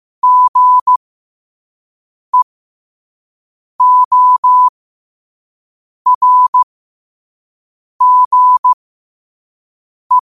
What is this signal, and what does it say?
Morse code, heard as a keyed tone, about 15 wpm, spaced as for 5 wpm, 1000 Hz. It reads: GEORGE
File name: qotdfiles / 5wpm